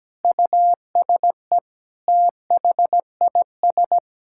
Morse code, heard as a keyed tone, about 17 wpm, 700 Hz. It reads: USE THIS
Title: Sentences from Top 1000 Words - Review Part 5 - 17wpm